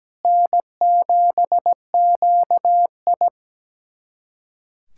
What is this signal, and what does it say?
Morse code, heard as a keyed tone, about 17 wpm, 700 Hz. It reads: N7QI
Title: USA Callsign DE Callsign - 4 Characters - 17wpm